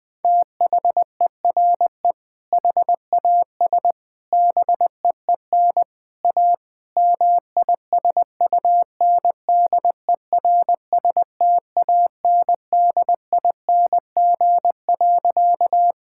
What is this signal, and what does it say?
Morse code, heard as a keyed tone, about 20 wpm, 700 Hz. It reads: THERE HAS BEEN A MISUNDERSTANDING.
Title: Sentences with 4-Word Phrases - 20wpm